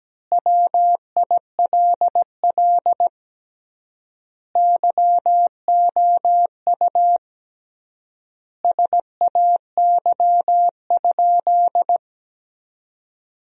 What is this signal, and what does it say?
Morse code, heard as a keyed tone, about 17 wpm, 700 Hz. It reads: WILL YOU SAY?
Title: 3x Word-Spacing - Sentences from Top 200 Words - 17wpm